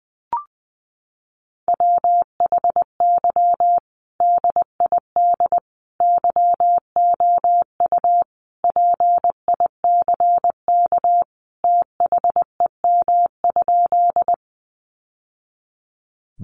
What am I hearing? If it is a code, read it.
Morse code, heard as a keyed tone, about 19 wpm, 700 Hz. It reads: WHY DID YOU PICK THEM?
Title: Sentences from Top 1000 Words 20wpm